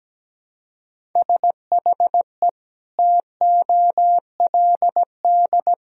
Morse code, heard as a keyed tone, about 17 wpm, 700 Hz. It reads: SHE TOLD
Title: Sentences from Top 400 Words and Abbreviations 17wpm